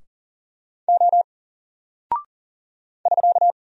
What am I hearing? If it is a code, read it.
Morse code, heard as a keyed tone, about 40 wpm, 700 Hz. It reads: O <SK>
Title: Single Character (Letter:Number:Punctuation:Prosign) Alt 40wpm